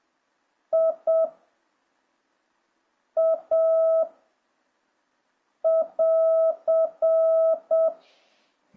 Morse code, heard as a keyed tone, about 7 wpm, 650 Hz. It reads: I A <AR>